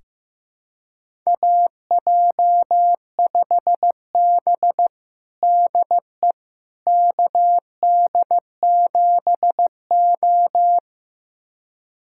Morse code, heard as a keyed tone, about 16 wpm, 700 Hz. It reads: AJ5B DE KD7O
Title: USA Callsign DE Callsign - 4 Characters - 15wpm